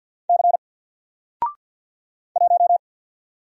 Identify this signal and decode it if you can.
Morse code, heard as a keyed tone, about 48 wpm, 700 Hz. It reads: C 1
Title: Single Letter-Number Alt 50wpm